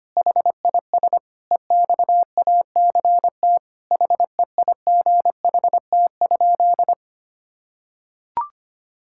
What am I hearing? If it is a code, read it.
Morse code, heard as a keyed tone, about 25 wpm, 700 Hz. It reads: HIS EXACT HEIGHT?
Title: Sentences from Top 1600 Words 25wpm